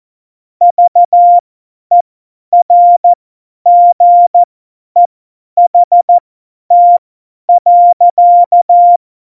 Morse code, heard as a keyed tone, about 14 wpm, spaced as for 10 wpm, 700 Hz. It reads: VERGEHT.
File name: Morsekurs / Witze / Witz 10.Juli 2018, 50 F70